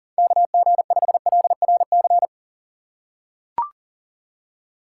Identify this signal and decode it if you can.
Morse code, heard as a keyed tone, about 40 wpm, 700 Hz. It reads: KG5LRC
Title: USA Call Signs - 6 Characters - 4x Character Spacing - 40wpm